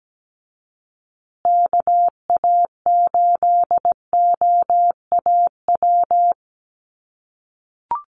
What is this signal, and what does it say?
Morse code, heard as a keyed tone, about 17 wpm, 700 Hz. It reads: KA8OAW
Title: USA Call Signs Repeated 3x 17wpm